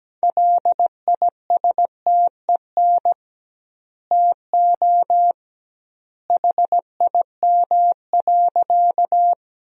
Morse code, heard as a keyed tone, about 17 wpm, 700 Hz. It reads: LISTEN TO HIM.
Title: 2x Word-Spacing - Sentences from Top 400 Words - 17wpm